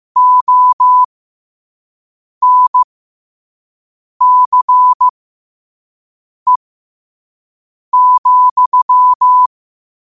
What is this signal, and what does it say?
Morse code, heard as a keyed tone, about 15 wpm, spaced as for 5 wpm, 1000 Hz. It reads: ONCE,
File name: qotdfiles / 5wpm